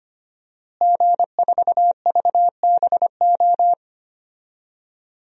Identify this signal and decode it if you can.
Morse code, heard as a keyed tone, about 25 wpm, 700 Hz. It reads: G4VBO
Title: DXCC-100 Call Signs - 5 Characters - Repeated 3x - 25wpm